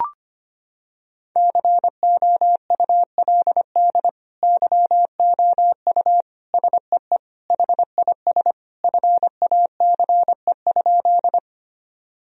Morse code, heard as a keyed tone, about 25 wpm, 700 Hz. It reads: COULD YOU SEE HIS FACE?
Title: Sentences from Top 1000 Words - Review Part 2 - 25wpm